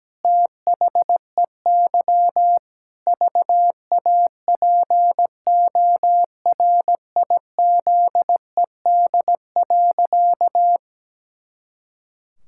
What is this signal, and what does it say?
Morse code, heard as a keyed tone, about 17 wpm, 700 Hz. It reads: THEY VAPORIZED.
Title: Pangram Challenge v2 - 17wpm